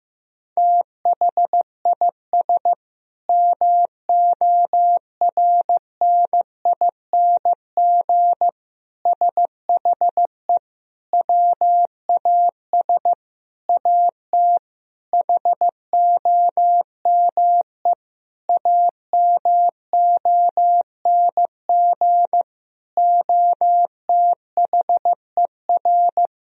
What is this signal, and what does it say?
Morse code, heard as a keyed tone, about 15 wpm, 700 Hz. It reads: THIS MORNING SHE WAS AT HOME AMONG OTHER
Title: Sentences from Top 1000 Words - Review Part 3 - 15wpm